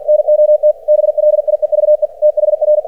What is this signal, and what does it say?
Morse code, heard as a keyed tone, about 50 wpm, 600 Hz. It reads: NOT BLIEVE THR
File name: archived / CRSnet